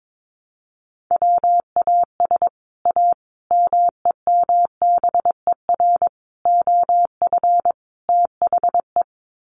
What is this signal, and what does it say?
Morse code, heard as a keyed tone, about 22 wpm, 700 Hz. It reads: WAS A MEMBER OF THE
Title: Sets of 5 Words 22wpm